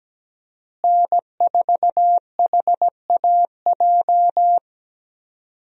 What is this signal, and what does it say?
Morse code, heard as a keyed tone, about 17 wpm, 700 Hz. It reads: N4HAJ
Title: USA Call Signs - 5 Characters - Repeated 3x - 17wpm